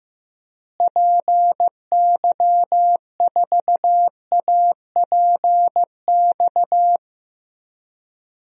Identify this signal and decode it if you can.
Morse code, heard as a keyed tone, about 15 wpm, 700 Hz. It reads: PY4APX
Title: DXCC-100 Call Signs - 6 Characters - Repeated 3x - 15wpm